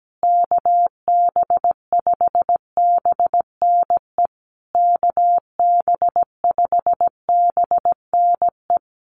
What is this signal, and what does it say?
Morse code, heard as a keyed tone, about 17 wpm, 700 Hz. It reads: KB5BNE KB5BNE
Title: USA Call Signs Repeated 3x 17wpm